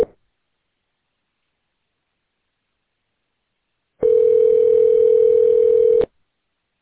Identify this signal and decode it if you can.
Ringback tone (North America), 440 + 480 Hz.